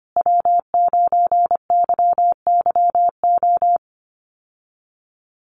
Morse code, heard as a keyed tone, about 25 wpm, 700 Hz. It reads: W9YYO
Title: USA Call Signs Repeated 2x 25wpm